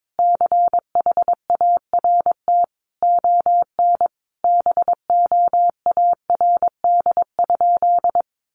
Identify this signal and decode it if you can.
Morse code, heard as a keyed tone, about 22 wpm, 700 Hz. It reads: CHART ON BOARD?
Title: Sentences from Top 1300 Words 22wpm